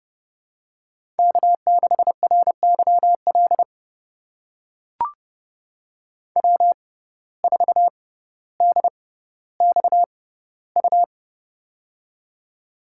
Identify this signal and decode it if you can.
Morse code, heard as a keyed tone, about 30 wpm, spaced as for 15 wpm, 700 Hz. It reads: K6RYL W4DXU
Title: USA Call Signs - 5 Characters - 6x Character Spacing - 30wpm